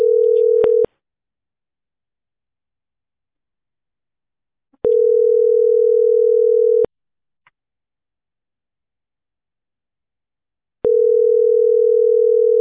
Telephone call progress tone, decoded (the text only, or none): ringback tone